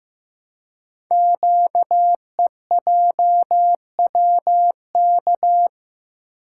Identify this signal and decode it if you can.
Morse code, heard as a keyed tone, about 15 wpm, 700 Hz. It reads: QEJWK
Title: Code Groups - 5 Letters - 15wpm